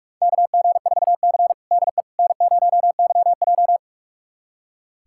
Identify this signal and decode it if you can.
Morse code, heard as a keyed tone, about 46 wpm, 700 Hz. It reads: KG4C DE N0YJ
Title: USA Callsign DE Callsign - 4 Characters - 45wpm